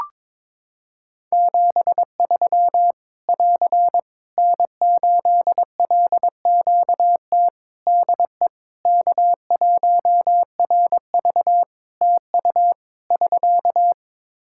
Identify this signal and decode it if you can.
Morse code, heard as a keyed tone, about 22 wpm, 700 Hz. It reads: 73 <AR> N8LQT DE K1RV TU <SK>
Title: K1USN SST QSO Practice 22wpm